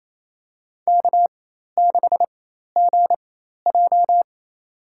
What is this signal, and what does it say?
Morse code, heard as a keyed tone, about 28 wpm, spaced as for 13 wpm, 700 Hz. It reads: K6GJ
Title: USA Call Signs - 4x Character Spacing - 28wpm